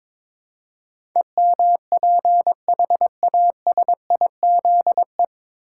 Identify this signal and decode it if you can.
Morse code, heard as a keyed tone, about 22 wpm, 700 Hz. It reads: EMPHASIZE
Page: Morse Code Ninja practice 22wpm